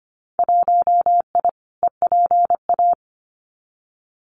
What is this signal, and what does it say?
Morse code, heard as a keyed tone, about 25 wpm, 700 Hz. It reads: 1I EPA
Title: Winter Field Day Exchanges Encore 25wpm